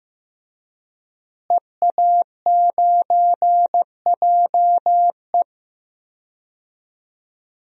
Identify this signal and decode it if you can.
Morse code, heard as a keyed tone, about 15 wpm, 700 Hz. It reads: EA9JE
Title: DXCC-100 Call Signs - 5 Characters - Repeated 2x - 15wpm